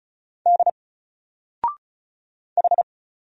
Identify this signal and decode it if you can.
Morse code, heard as a keyed tone, about 35 wpm, 700 Hz. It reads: D H
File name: Single-Letter-Number-35wpm